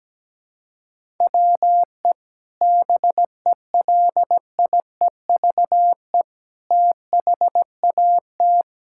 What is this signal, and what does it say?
Morse code, heard as a keyed tone, about 17 wpm, 700 Hz. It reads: WE BELIEVE THAT
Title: Sentences from Top 1000 Words - Review Part 6 - 17wpm